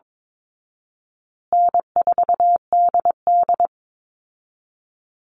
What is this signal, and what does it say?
Morse code, heard as a keyed tone, about 22 wpm, 700 Hz. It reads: N4DD
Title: Easy Call Signs Repeated 3x - 22wpm